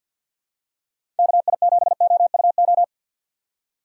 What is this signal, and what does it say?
Morse code, heard as a keyed tone, about 51 wpm, 700 Hz. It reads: KI7OUO
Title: USA Call Signs - 6 Characters - 8x Character Spacing - 50wpm